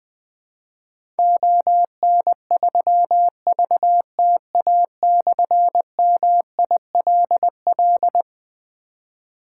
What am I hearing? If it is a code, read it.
Morse code, heard as a keyed tone, about 21 wpm, 700 Hz. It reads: ON3VTA/MILL
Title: Difficult Call Signs 20wpm